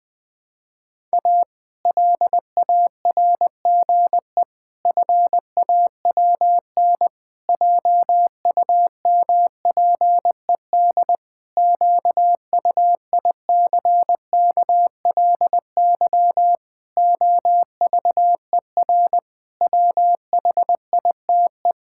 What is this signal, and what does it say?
Morse code, heard as a keyed tone, about 21 wpm, 700 Hz. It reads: A LARGE FAWN JUMPED QUICKLY OVER WHITE
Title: Pangram Challenge v2 - 20wpm